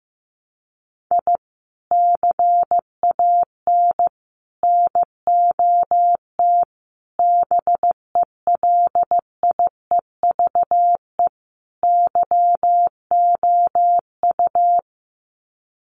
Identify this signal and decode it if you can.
Morse code, heard as a keyed tone, about 14 wpm, 700 Hz. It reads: I CAN NOT BELIEVE YOU
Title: Sets of 5 Words Encore 15wpm